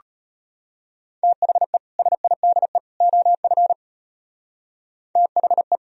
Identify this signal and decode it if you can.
Morse code, heard as a keyed tone, about 32 wpm, 700 Hz. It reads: THE SIDE OF THE